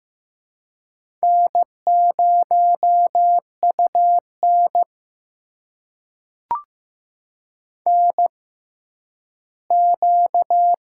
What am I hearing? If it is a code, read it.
Morse code, heard as a keyed tone, about 15 wpm, spaced as for 10 wpm, 700 Hz. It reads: N0UN NQ
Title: USA Call Signs - 4 Characters - 6x Character Spacing - 15wpm